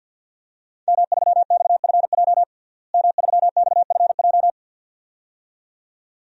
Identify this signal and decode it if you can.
Morse code, heard as a keyed tone, about 47 wpm, 700 Hz. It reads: M3XFJ M3XFJ
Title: DXCC-100 Call Signs - 5 Characters - Repeated 2x - 50wpm